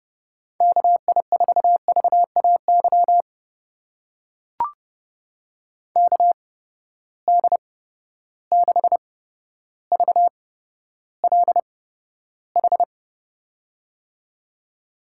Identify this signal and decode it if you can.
Morse code, heard as a keyed tone, about 31 wpm, spaced as for 13 wpm, 700 Hz. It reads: KI4VAY KD6VLH
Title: USA Call Signs - 6 Characters - 8x Character Spacing - 30wpm